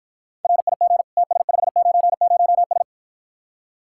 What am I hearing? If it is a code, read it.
Morse code, heard as a keyed tone, about 52 wpm, 700 Hz. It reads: RIG TS590S